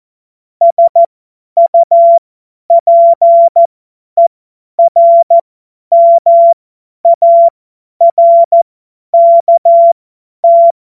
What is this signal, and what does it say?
Morse code, heard as a keyed tone, about 14 wpm, spaced as for 10 wpm, 680 Hz. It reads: SUPERMARKT
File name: Morsekurs / Witze / Witz 20. Januar 2026 50 F70